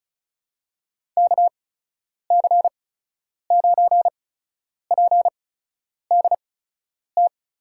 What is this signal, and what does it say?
Morse code, heard as a keyed tone, about 35 wpm, spaced as for 10 wpm, 700 Hz. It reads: KC9PDT
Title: USA Call Signs - 6 Characters - 8x Character Spacing - 35wpm